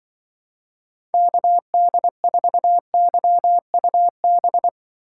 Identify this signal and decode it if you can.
Morse code, heard as a keyed tone, about 24 wpm, 700 Hz. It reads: KD4YUB
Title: Speed Racing - USA 2x3 Call Signs 20wpm